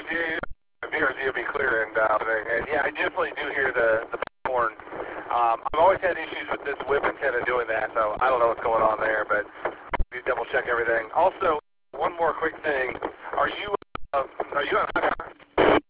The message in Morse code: none